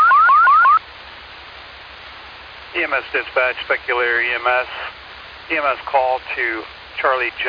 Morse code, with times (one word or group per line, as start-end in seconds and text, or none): none